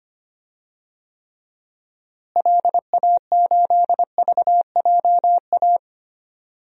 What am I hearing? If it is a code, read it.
Morse code, heard as a keyed tone, about 25 wpm, 700 Hz. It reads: LA8VJA
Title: DXCC-100 Call Signs - 6 Characters - Repeated 3x - 25wpm